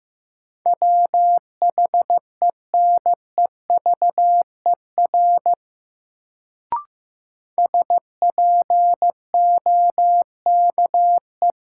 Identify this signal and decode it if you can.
Morse code, heard as a keyed tone, about 15 wpm, 700 Hz. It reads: WHENEVER SPOKE
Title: Sentences from Top 1200 Words - New Word Drill Alt - 15wpm